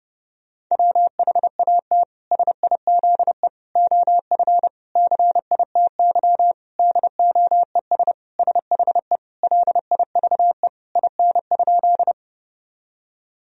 Morse code, heard as a keyed tone, about 30 wpm, 700 Hz. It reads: WHAT SIZE OF CITY DOES SHE LIVE IN?
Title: Sentences from Top 1000 Words - Review Part 2 - 30wpm